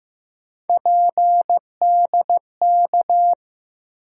Morse code, heard as a keyed tone, about 15 wpm, 700 Hz. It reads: PDK
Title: USA Call Sign Postfixes - 3 Letters 15wpm